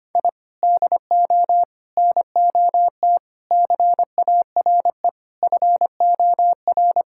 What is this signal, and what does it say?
Morse code, heard as a keyed tone, about 25 wpm, 700 Hz. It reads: I DO NOT CARE FOR
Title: Sentences from Top 1000 Words - Review Part 6 - 25wpm